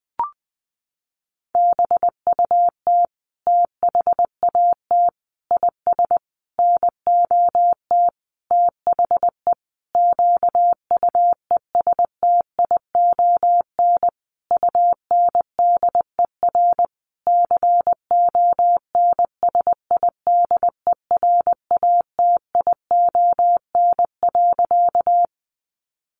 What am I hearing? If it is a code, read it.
Morse code, heard as a keyed tone, about 20 wpm, 700 Hz. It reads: BUT THAT IS NOT THE QUESTION UNDER CONSIDERATION.
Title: Sentences from Top 1500 Words 20wpm